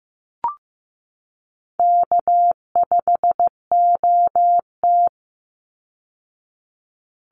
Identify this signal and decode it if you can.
Morse code, heard as a keyed tone, about 15 wpm, 700 Hz. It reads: K5OT
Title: Easy Call Signs Repeated 2x - 15wpm